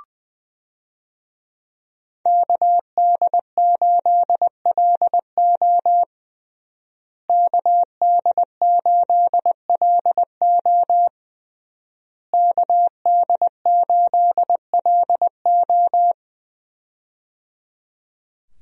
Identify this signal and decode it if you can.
Morse code, heard as a keyed tone, about 20 wpm, 700 Hz. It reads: KD8LO KD8LO KD8LO
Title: USA Call Signs - 5 Characters - Repeated 3x - 20wpm